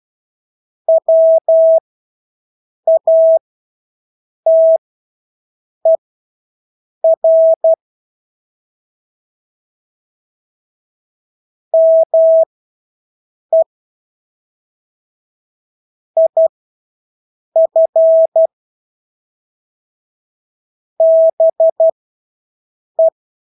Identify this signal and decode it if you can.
Morse code, heard as a keyed tone, about 12 wpm, spaced as for 6 wpm, 650 Hz. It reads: WATER ME IF BE
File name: Engelske ord 12-6 wpm